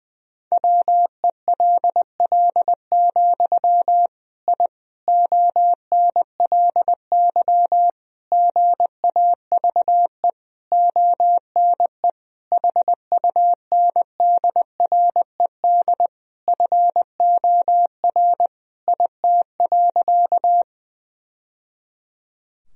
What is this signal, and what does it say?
Morse code, heard as a keyed tone, about 20 wpm, 700 Hz. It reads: WELL, I ONLY GAVE ONE HUNDRED FOR IT.
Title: Sentences from Top 1000 Words - Review Part 2 - 20wpm